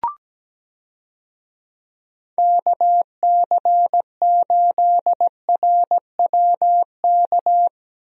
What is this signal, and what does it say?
Morse code, heard as a keyed tone, about 17 wpm, 700 Hz. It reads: KC8RWK
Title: USA Call Signs - 6 Characters - Repeated 2x - 17wpm